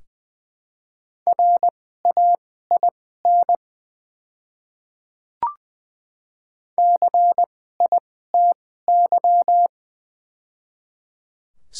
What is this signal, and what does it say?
Morse code, heard as a keyed tone, about 20 wpm, spaced as for 14 wpm, 700 Hz. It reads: RAIN CITY
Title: 2x Character Spacing - Top 500 Words - 20wpm